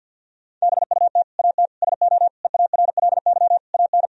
Morse code, heard as a keyed tone, about 50 wpm, 700 Hz. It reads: BUT AT SO EARLY AN
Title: Sentences from Top 1000 Words - Review Part 3 - 50wpm